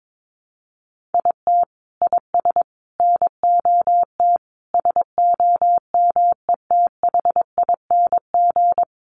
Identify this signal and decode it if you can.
Morse code, heard as a keyed tone, about 22 wpm, 700 Hz. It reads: IT IS NOT SOMETHING
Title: Sentences from Top 1400 Words 22wpm